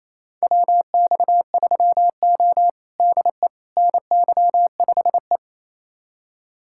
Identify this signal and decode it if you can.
Morse code, heard as a keyed tone, about 28 wpm, 700 Hz. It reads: WX3O DE NY5E
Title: USA Callsign DE Callsign - 4 Characters - 28wpm